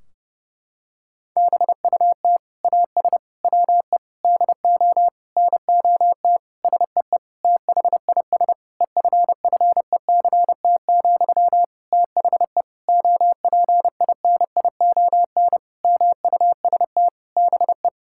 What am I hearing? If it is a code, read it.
Morse code, heard as a keyed tone, about 30 wpm, 700 Hz. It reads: BUT AS WE DO NOT SEE THIS EFFECT, THE OPINION MUST BE
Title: Sentences from Top 1600 Words 30wpm